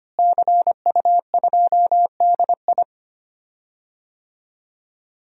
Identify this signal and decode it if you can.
Morse code, heard as a keyed tone, about 25 wpm, 700 Hz. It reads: CU2DI
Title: DXCC-100 Call Signs - 5 Characters - Repeated 2x - 25wpm